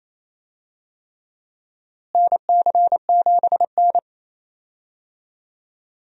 Morse code, heard as a keyed tone, about 28 wpm, 700 Hz. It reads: NC7N
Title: USA Call Signs - 4 Characters - Repeated 3x - 28wpm